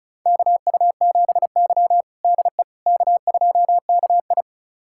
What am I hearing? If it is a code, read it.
Morse code, heard as a keyed tone, about 36 wpm, 700 Hz. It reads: KU7Y DE K2KI
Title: USA Callsign DE Callsign - 4 Characters - 35wpm